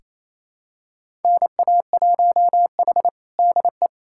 Morse code, heard as a keyed tone, about 28 wpm, 700 Hz. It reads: NA1H DE
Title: USA Callsign DE Callsign - 4 Characters - 28wpm